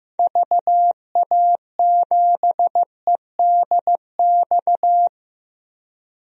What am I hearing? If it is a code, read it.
Morse code, heard as a keyed tone, about 15 wpm, 700 Hz. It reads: VA7EDX
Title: DXCC-100 Call Signs - 6 Characters - 15wpm